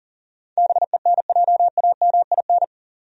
Morse code, heard as a keyed tone, about 38 wpm, 700 Hz. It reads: BENJAMIN
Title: USA Common Names - 40wpm